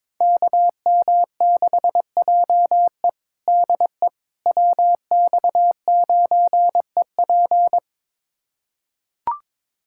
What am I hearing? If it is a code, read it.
Morse code, heard as a keyed tone, about 22 wpm, 700 Hz. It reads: KM6JE DE WX9EP
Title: USA Callsign DE Callsign 22wpm